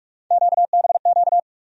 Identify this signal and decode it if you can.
Morse code, heard as a keyed tone, about 47 wpm, 700 Hz. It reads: QBQ
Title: USA Call Sign Postfixes - 3 Letters 45wpm